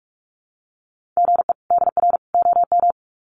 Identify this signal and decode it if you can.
Morse code, heard as a keyed tone, about 44 wpm, 700 Hz. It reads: GE DR OM
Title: Common QSO Phrases 45wpm